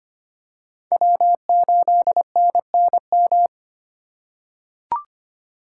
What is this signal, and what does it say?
Morse code, heard as a keyed tone, about 25 wpm, 700 Hz. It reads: W8NNM
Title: USA Call Signs - 5 Characters - 3x Character Spacing - 25wpm